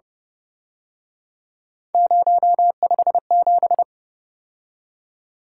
Morse code, heard as a keyed tone, about 29 wpm, 700 Hz. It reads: 057
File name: Speed-Racing - Three Letter-Number Combinations - 25wpm